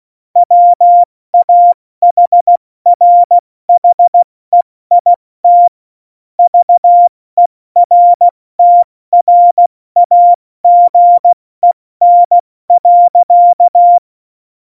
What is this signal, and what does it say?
Morse code, heard as a keyed tone, about 16 wpm, 700 Hz. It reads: WAHRHEIT VERTRAGEN.
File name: Morsekurs / Witze / Witz 30. Januar 2018, 70 F80